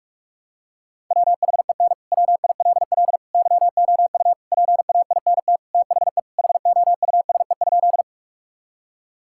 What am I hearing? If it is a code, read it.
Morse code, heard as a keyed tone, about 46 wpm, 700 Hz. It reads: WHEN WILL YOU PAINT THE HOUSE?
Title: Sentences from Top 1000 Words - Review Part 6 - 45wpm